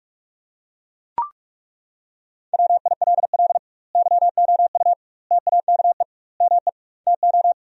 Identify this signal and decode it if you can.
Morse code, heard as a keyed tone, about 42 wpm, 700 Hz. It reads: WILL YOU TAKE ME TO
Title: 2x Word-Spacing - Sentences from Top 200 Words - 45wpm